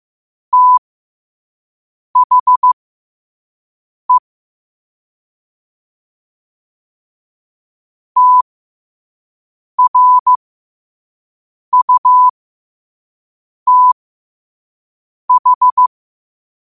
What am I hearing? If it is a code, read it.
Morse code, heard as a keyed tone, about 15 wpm, spaced as for 5 wpm, 1000 Hz. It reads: THE TRUTH
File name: qotdfiles / 5wpm